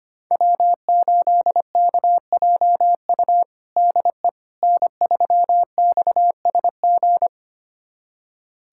Morse code, heard as a keyed tone, about 25 wpm, 700 Hz. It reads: W8KJU DE N3XSG
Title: USA Callsign DE Callsign - 5 Characters - 25wpm